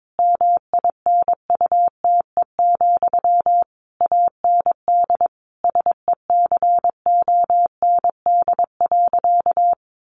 Morse code, heard as a keyed tone, about 22 wpm, 700 Hz. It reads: MINUTE, AND SECOND.
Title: A Sound Of Thunder 22wpm